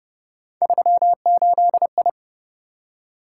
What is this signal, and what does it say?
Morse code, heard as a keyed tone, about 28 wpm, 700 Hz. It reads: 38I